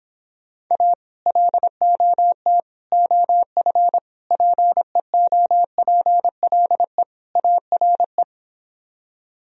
Morse code, heard as a keyed tone, about 26 wpm, 700 Hz. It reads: A LOT OF PEOPLE ARE